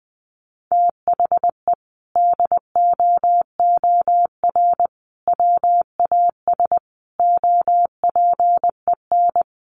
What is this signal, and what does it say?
Morse code, heard as a keyed tone, about 19 wpm, 700 Hz. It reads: THE DOOR WAS OPEN
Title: Sentences from Top 1300 Words 20wpm